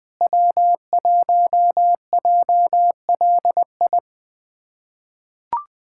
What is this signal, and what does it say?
Morse code, heard as a keyed tone, about 20 wpm, 700 Hz. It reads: W1JLI
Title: USA Callsign DE Callsign - 5 Characters - 20wpm